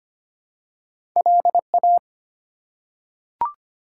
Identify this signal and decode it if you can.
Morse code, heard as a keyed tone, about 25 wpm, 700 Hz. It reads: LA
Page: morse-code-ninja-practice-25wpm